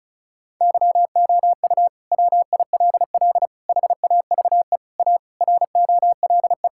Morse code, heard as a keyed tone, about 34 wpm, 700 Hz. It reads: YOU WILL HAVE A ROLE
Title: Sentences from Top 1000 Words - Review Part 6 - 35wpm